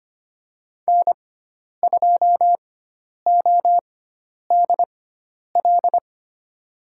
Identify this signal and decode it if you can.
Morse code, heard as a keyed tone, about 25 wpm, spaced as for 10 wpm, 700 Hz. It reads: N2ODL
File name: Lesson 16g - Letter c - Call Signs 25wpm